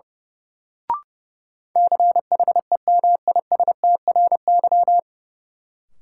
Morse code, heard as a keyed tone, about 30 wpm, 700 Hz. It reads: CHEMISTRY
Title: Sentences from Top 1000 Words - Review Part 6 - New Word Drill - 30wpm